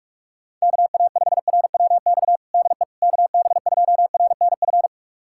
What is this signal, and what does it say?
Morse code, heard as a keyed tone, about 46 wpm, 700 Hz. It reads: KA5RWX DE KB2RNF
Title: USA Callsign DE Callsign - 6 Characters - 45wpm